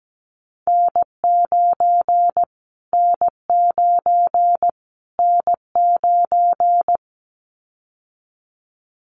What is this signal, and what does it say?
Morse code, heard as a keyed tone, about 17 wpm, 700 Hz. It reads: N9 N9 N9
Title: Callsign Prefixes for Top 25 DXCC - Repeated 3x - 17wpm